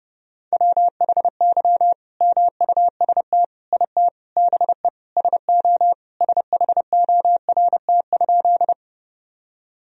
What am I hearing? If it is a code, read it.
Morse code, heard as a keyed tone, about 30 wpm, 700 Hz. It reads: WHY MUST IT BE SO SHORT?
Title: Sentences from Top 1000 Words - Review Part 2 - 30wpm